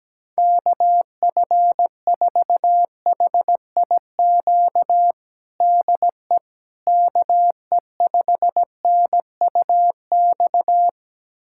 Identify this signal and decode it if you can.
Morse code, heard as a keyed tone, about 17 wpm, 700 Hz. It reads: KF4HIQ DE KE5NUX
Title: USA Callsign DE Callsign 17wpm